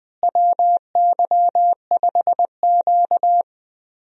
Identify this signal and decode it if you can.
Morse code, heard as a keyed tone, about 20 wpm, 700 Hz. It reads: WY5Q
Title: USA Call Signs - 4 Characters - Repeated 2x - 20wpm